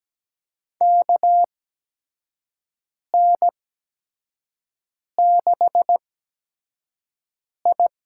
Morse code, heard as a keyed tone, about 17 wpm, spaced as for 5 wpm, 700 Hz. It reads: KN6I
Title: USA Call Signs - 5 Characters - 8x Character Spacing - 17wpm